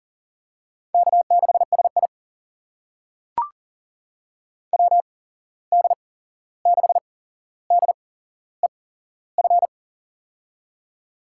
Morse code, heard as a keyed tone, about 40 wpm, spaced as for 15 wpm, 700 Hz. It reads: K6SI WD6DEF
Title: USA Call Signs - 8x Character Spacing - 40wpm